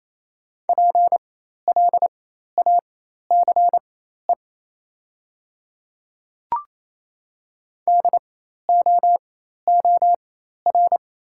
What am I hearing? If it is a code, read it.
Morse code, heard as a keyed tone, about 28 wpm, spaced as for 13 wpm, 700 Hz. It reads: PLACE DOOR
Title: 4x Character Spacing - Top 300 Words - 28wpm